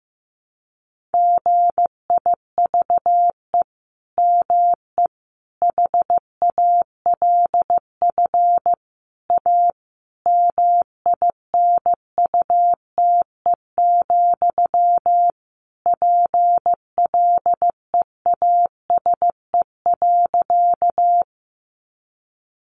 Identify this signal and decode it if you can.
Morse code, heard as a keyed tone, about 15 wpm, 700 Hz. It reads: GIVE ME HALF A MINUTE, PLEASE.
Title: Sentences from Top 1500 Words 15wpm